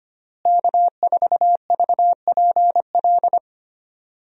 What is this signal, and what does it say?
Morse code, heard as a keyed tone, about 25 wpm, 700 Hz. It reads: K4VPL
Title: USA Call Signs - 5 Characters - Repeated 3x - 25wpm